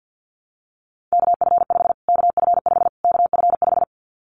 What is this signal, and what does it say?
Morse code, heard as a keyed tone, about 50 wpm, 700 Hz. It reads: KF5 KF5 KF5
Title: Callsign Prefixes for Top 25 DXCC - Repeated 3x - 50wpm